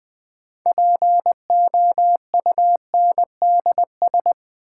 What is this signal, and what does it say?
Morse code, heard as a keyed tone, about 20 wpm, 700 Hz. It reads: POUNDS
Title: Sentences from Top 1000 Words - Review Part 2 - New Word Drill - 20wpm